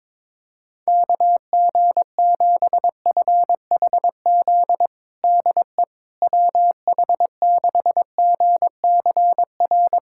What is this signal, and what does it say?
Morse code, heard as a keyed tone, about 22 wpm, 700 Hz. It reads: KG7FHZ DE WH6GCR
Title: USA Callsign DE Callsign - 6 Characters - 22wpm